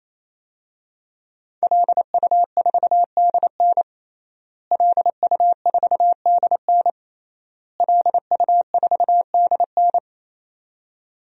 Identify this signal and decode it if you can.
Morse code, heard as a keyed tone, about 28 wpm, 700 Hz. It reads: LU4DN LU4DN LU4DN
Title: DXCC-100 Call Signs - 5 Characters - Repeated 3x - 28wpm